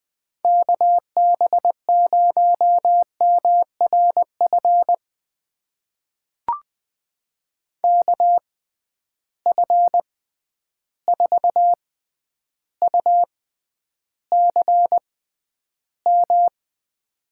Morse code, heard as a keyed tone, about 20 wpm, spaced as for 10 wpm, 700 Hz. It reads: KB0MRF KF4UCM
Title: USA Call Signs - 6 Characters - 6x Character Spacing - 20wpm